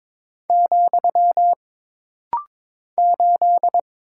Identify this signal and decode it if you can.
Morse code, heard as a keyed tone, about 22 wpm, 700 Hz. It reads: , 8
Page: This rapid-fire timing is geared to push your ICR (Instant Character Recognition) skills.